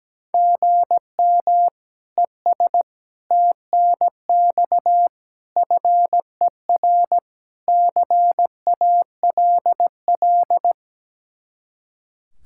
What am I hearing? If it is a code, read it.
Morse code, heard as a keyed tone, about 17 wpm, 700 Hz. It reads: GM ES TNX FER CALL
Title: Practice QSO - Exchange 1 - Beginning - 17wpm